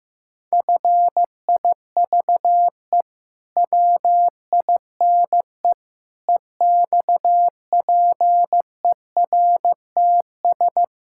Morse code, heard as a keyed tone, about 15 wpm, 700 Hz. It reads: FIVE WINE EXPERTS
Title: Pangram Challenge v2 - 15wpm